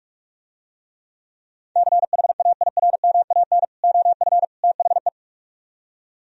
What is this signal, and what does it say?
Morse code, heard as a keyed tone, about 49 wpm, 700 Hz. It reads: CHAIRMAN OF THE